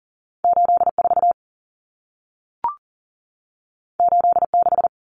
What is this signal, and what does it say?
Morse code, heard as a keyed tone, about 40 wpm, 700 Hz. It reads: 84 86
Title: Numbers - 2-Digit 40wpm